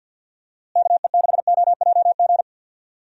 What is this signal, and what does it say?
Morse code, heard as a keyed tone, about 48 wpm, 700 Hz. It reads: KE6OJG
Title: USA Call Signs - 6 Characters - Repeated 3x - 50wpm